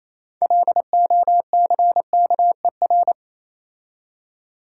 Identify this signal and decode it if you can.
Morse code, heard as a keyed tone, about 28 wpm, 700 Hz. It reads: LOCKER